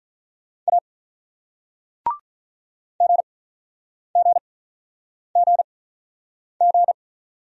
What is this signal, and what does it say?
Morse code, heard as a keyed tone, about 48 wpm, spaced as for 9 wpm, 700 Hz. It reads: A GGGG